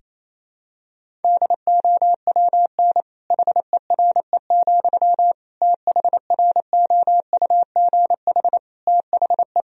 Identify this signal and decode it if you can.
Morse code, heard as a keyed tone, about 28 wpm, 700 Hz. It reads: DOWN HERE, THROUGH THE
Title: Sentences from Top 1000 Words - Review Part 1 - 28wpm